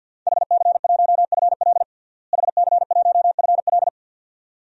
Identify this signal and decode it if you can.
Morse code, heard as a keyed tone, about 52 wpm, 700 Hz. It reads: HC1FL HC1FL
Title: DXCC-100 Call Signs - 5 Characters - Repeated 2x - 50wpm